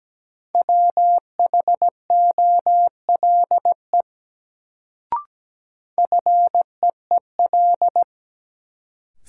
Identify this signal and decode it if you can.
Morse code, heard as a keyed tone, about 17 wpm, 700 Hz. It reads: WHOLE FEEL
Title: Sentences from Top 400 Words - New Word Drill - 17wpm